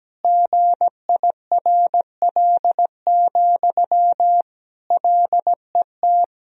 Morse code, heard as a keyed tone, about 17 wpm, 700 Hz. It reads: GIRL, LET
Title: Sentences from Top 400 Words and Abbreviations 17wpm